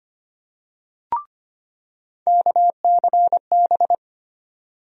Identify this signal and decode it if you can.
Morse code, heard as a keyed tone, about 25 wpm, 700 Hz. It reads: KCB
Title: USA Call Sign Postfixes - 3 Letters 25wpm